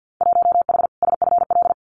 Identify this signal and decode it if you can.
Morse code, heard as a keyed tone, about 50 wpm, 700 Hz. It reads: 1H SFL
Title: Winter Field Day Exchanges 50wpm